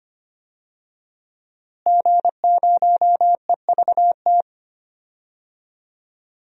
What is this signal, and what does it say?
Morse code, heard as a keyed tone, about 25 wpm, 700 Hz. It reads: G0EVT
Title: DXCC-100 Call Signs - 5 Characters - Repeated 3x - 25wpm